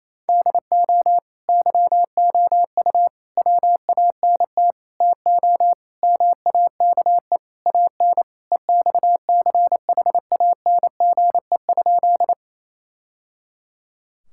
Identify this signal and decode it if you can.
Morse code, heard as a keyed tone, about 28 wpm, 700 Hz. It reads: DO YOU WANT TO MAKE AN EXCHANGE?
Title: Sentences from Top 1000 Words - Review Part 6 - 28wpm